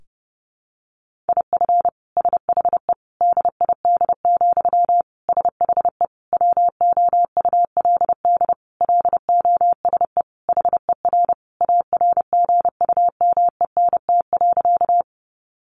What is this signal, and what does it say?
Morse code, heard as a keyed tone, about 28 wpm, 700 Hz. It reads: IF SHE DID, SHE WOULD LOSE HER ARGUMENT.
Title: Sentences from Top 1400 Words 30wpm